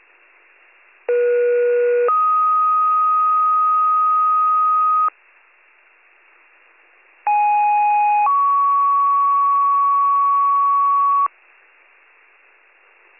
Two-tone page, heard as a one-second tone, then a three-second tone, about 480 then 1250 Hz, about 830 then 1120 Hz. It is heard from 1.1 to 11.3 s.